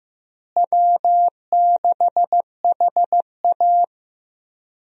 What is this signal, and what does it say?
Morse code, heard as a keyed tone, about 15 wpm, 700 Hz. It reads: W6HA
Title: USA Call Signs - 4 Characters - Repeated 3x - 15wpm